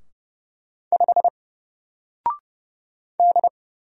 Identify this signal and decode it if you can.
Morse code, heard as a keyed tone, about 30 wpm, 700 Hz. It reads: 5 D
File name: Single-Letter-Number-30wpm